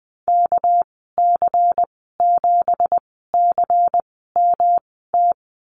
Morse code, heard as a keyed tone, about 20 wpm, spaced as for 15 wpm, 700 Hz. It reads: KC7CMT
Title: 200 Lesson 32g - Letter k - Call Signs 20wpm